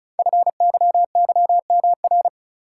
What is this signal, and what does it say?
Morse code, heard as a keyed tone, about 35 wpm, 700 Hz. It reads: FYYMR
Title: Code Groups - 5 Letter:Numbers - 35wpm